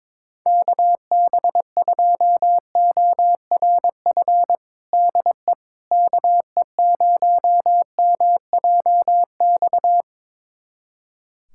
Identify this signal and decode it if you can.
Morse code, heard as a keyed tone, about 22 wpm, 700 Hz. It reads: KB2ORF DE KE0MJX
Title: USA Callsign DE Callsign - 6 Characters - 22wpm